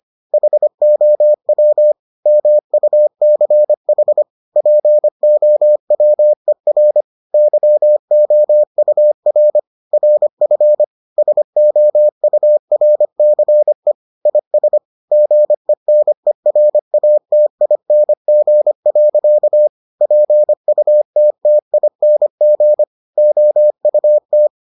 Morse code, heard as a keyed tone, about 25 wpm, 600 Hz. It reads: HOW MUCH POWER YOUR RF SOURCE IS GENERATING. PUTTING OUT